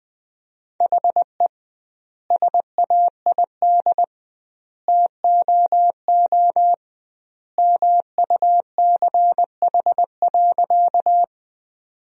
Morse code, heard as a keyed tone, about 21 wpm, 700 Hz. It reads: HE SAID TOO MUCH.
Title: 2x Word-Spacing - Sentences from Top 200 Words - 20wpm